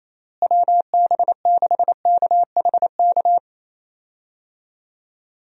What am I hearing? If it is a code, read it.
Morse code, heard as a keyed tone, about 28 wpm, 700 Hz. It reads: WB6KHK
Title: USA Call Signs - 6 Characters - Repeated 3x - 28wpm